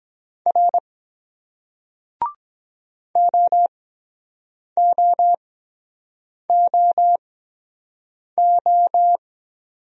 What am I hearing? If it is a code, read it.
Morse code, heard as a keyed tone, about 23 wpm, spaced as for 7 wpm, 700 Hz. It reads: R OOOO